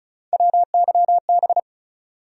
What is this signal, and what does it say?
Morse code, heard as a keyed tone, about 34 wpm, 700 Hz. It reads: WYB
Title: USA Call Sign Postfixes - 3 Letters 35wpm